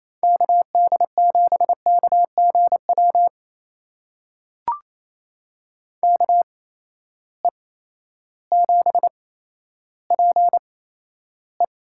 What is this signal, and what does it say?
Morse code, heard as a keyed tone, about 28 wpm, spaced as for 13 wpm, 700 Hz. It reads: KD7KGW KE7PE
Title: USA Call Signs - 6 Characters - 8x Character Spacing - 28wpm